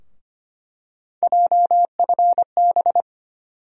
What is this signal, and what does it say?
Morse code, heard as a keyed tone, about 25 wpm, 700 Hz. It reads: JFB
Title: USA Call Sign Postfixes - 3 Letters 25wpm